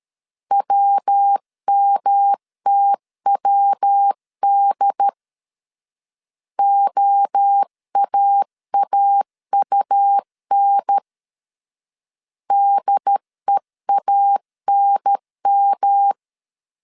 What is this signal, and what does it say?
Morse code, heard as a keyed tone, about 13 wpm, 800 Hz. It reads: WMTWD OAAUN DEANM
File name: CW / DET / 1 3 1500C S A 12wpm 2905